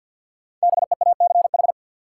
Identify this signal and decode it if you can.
Morse code, heard as a keyed tone, about 53 wpm, 700 Hz. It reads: BEACH
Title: Sentences from Top 1500 Words - New Word Drill - 50wpm